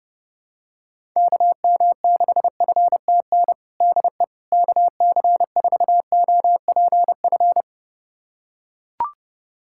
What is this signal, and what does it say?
Morse code, heard as a keyed tone, about 32 wpm, 700 Hz. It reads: KM6FTN DE KC4OPF
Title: USA Callsign DE Callsign 30wpm